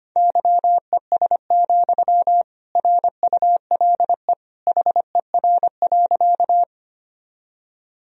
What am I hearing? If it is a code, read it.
Morse code, heard as a keyed tone, about 25 wpm, 700 Hz. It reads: YES, RULE HER.
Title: Sentences from Top 1000 Words - Review Part 2 - 25wpm